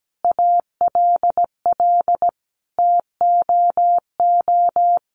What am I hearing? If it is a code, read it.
Morse code, heard as a keyed tone, about 17 wpm, 700 Hz. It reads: ALL TOO
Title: Sentences from Top 700 Words 17wpm